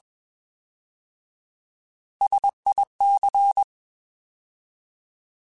Morse code, heard as a keyed tone, about 21 wpm, 800 Hz. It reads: SIC
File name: Alphabetical / Alphabetical mp3s / Ss @20